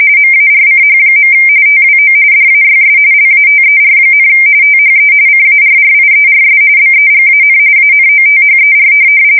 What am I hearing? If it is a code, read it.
RTTY, about 45.45 baud. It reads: MARCH. TWO DAYS LATER, THE CARRIERS BEGAN ANOTHER